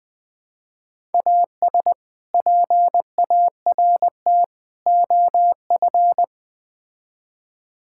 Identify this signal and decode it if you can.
Morse code, heard as a keyed tone, about 21 wpm, 700 Hz. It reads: AS PART OF
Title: Build A Phrase - 3-Word Phrases - 20wpm